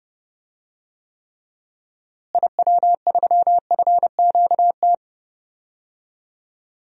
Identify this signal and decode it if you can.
Morse code, heard as a keyed tone, about 30 wpm, 700 Hz. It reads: IW3FQT
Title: DXCC-100 Call Signs - 6 Characters - Repeated 3x - 30wpm